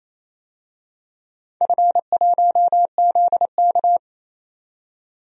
Morse code, heard as a keyed tone, about 28 wpm, 700 Hz. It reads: F1ZK
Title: DXCC-100 Call Signs - 4 Characters - 28wpm